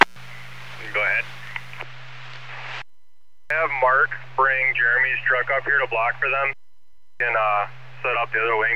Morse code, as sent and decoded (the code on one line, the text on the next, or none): none